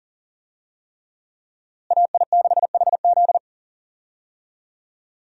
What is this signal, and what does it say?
Morse code, heard as a keyed tone, about 40 wpm, 700 Hz. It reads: AI6HZ
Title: USA Call Signs - 5 Characters - Repeated 3x - 40wpm